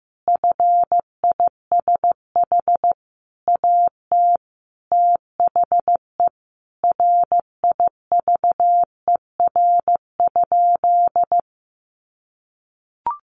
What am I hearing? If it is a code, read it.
Morse code, heard as a keyed tone, about 15 wpm, 700 Hz. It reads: FISH AT THE RIVER?
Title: Sentences from Top 1400 Words 15wpm